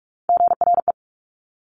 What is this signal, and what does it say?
Morse code, heard as a keyed tone, about 46 wpm, 700 Hz. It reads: GRE
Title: Top Three Letter Combinations in Words 45wpm